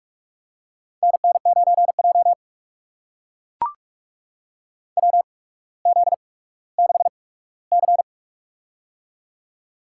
Morse code, heard as a keyed tone, about 44 wpm, spaced as for 19 wpm, 690 Hz. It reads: NN9J WZ6C
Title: USA Call Signs - 4 Characters - 8x Character Spacing - 45wpm